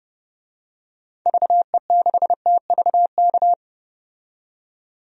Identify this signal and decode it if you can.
Morse code, heard as a keyed tone, about 30 wpm, 700 Hz. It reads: VE6TVK
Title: DXCC-100 Call Signs - 6 Characters - 30wpm